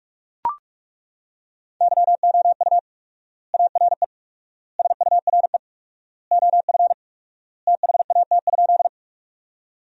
Morse code, heard as a keyed tone, about 41 wpm, 700 Hz. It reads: YOU ARE SURE OF THAT?
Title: 4x Word-Spacing - Sentences from Top 300 Words - 45wpm